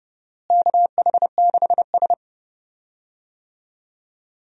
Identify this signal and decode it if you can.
Morse code, heard as a keyed tone, about 30 wpm, 700 Hz. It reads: KH6S
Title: DXCC-100 Call Signs - 4 Characters - Repeated 3x - 30wpm